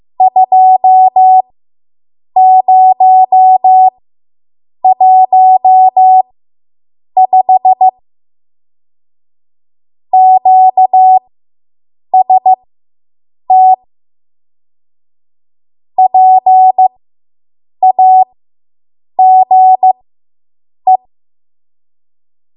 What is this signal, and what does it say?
Morse code, heard as a keyed tone, about 15 wpm, spaced as for 7 wpm, 750 Hz. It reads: 2015 QST PAGE